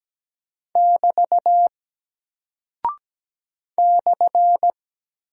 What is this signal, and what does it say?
Morse code, heard as a keyed tone, about 17 wpm, 700 Hz. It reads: <BT> /
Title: Single Character (Letter:Number:Punctuation:Prosign) Alt 17wpm